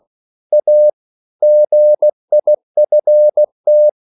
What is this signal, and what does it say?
Morse code, heard as a keyed tone, about 16 wpm, 600 Hz. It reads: A GIFT
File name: CW / Practice / EnglishSayings16